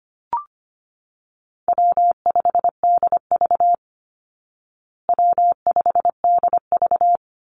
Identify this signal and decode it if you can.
Morse code, heard as a keyed tone, about 25 wpm, 700 Hz. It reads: W5DV W5DV
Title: Call Signs Repeated 3x - 25wpm